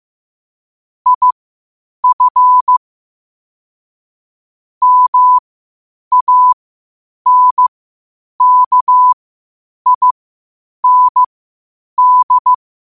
Morse code, heard as a keyed tone, about 15 wpm, spaced as for 8 wpm, 1000 Hz. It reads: IF MANKIND
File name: qotdfiles / 8wpm